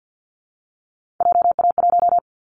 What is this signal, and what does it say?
Morse code, heard as a keyed tone, about 50 wpm, 700 Hz. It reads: JA1
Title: Callsign Prefixes for Top 50 DXCC - 50wpm